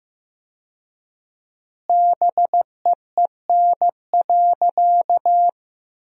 Morse code, heard as a keyed tone, about 14 wpm, 700 Hz. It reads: BEEN.
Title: 4x Word-Spacing - Sentences from Top 300 Words - 15wpm